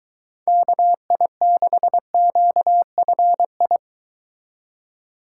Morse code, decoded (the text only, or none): KI6QFI